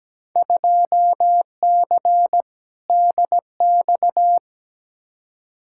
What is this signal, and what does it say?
Morse code, heard as a keyed tone, about 17 wpm, 700 Hz. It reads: 2C DX
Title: Field Day Exchanges Encore - 17wpm